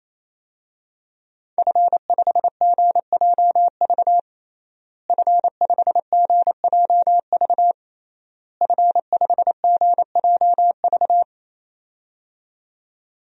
Morse code, heard as a keyed tone, about 27 wpm, 700 Hz. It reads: F5GJV F5GJV F5GJV
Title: DXCC-100 Call Signs - 5 Characters - Repeated 3x - 28wpm